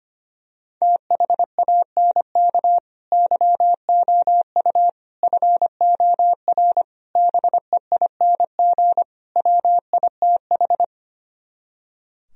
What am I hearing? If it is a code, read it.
Morse code, heard as a keyed tone, about 24 wpm, 700 Hz. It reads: THANK YOU FOR BEING WITH
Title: Speed-Racing - Sets of 5 Words 25wpm